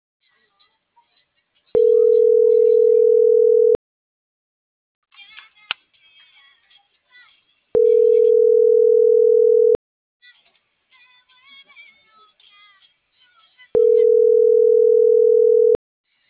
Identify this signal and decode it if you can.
Ringback tone (North America), 440 + 480 Hz.
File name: CENTRAL NMP / out patricia.martineznmp - .34935